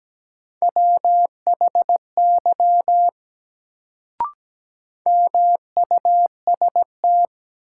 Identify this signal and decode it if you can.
Morse code, heard as a keyed tone, about 17 wpm, 700 Hz. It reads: WHY MUST
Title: Sentences from Top 200 Words - New Word Drill - 17wpm